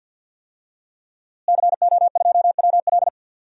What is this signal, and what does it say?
Morse code, heard as a keyed tone, about 51 wpm, 700 Hz. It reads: CO2WL
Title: DXCC-100 Call Signs - 5 Characters - Repeated 3x - 50wpm